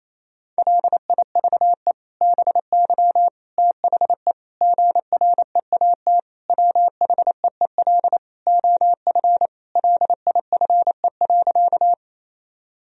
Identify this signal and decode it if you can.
Morse code, heard as a keyed tone, about 28 wpm, 700 Hz. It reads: LIVE BY THE GREAT WHEEL OF LIFE.
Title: Sentences from Top 1000 Words - Review Part 3 - 28wpm